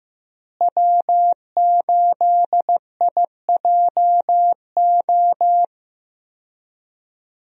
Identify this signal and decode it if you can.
Morse code, heard as a keyed tone, about 16 wpm, 700 Hz. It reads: W8IJO
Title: USA Call Signs - 5 Characters - Repeated 3x - 15wpm